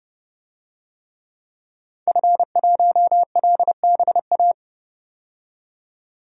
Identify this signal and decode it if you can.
Morse code, heard as a keyed tone, about 29 wpm, 700 Hz. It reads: F1LBA
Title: DXCC-100 Call Signs - 5 Characters - 30wpm